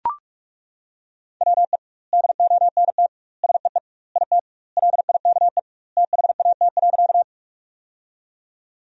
Morse code, heard as a keyed tone, about 42 wpm, 700 Hz. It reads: WE DONT SEE IT LIKE THAT.
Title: 2x Word-Spacing - Sentences from Top 300 Words - 45wpm